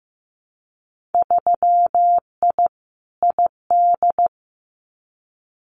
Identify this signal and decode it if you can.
Morse code, heard as a keyed tone, about 15 wpm, 700 Hz. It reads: 3I ID
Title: Winter Field Day Exchanges 15wpm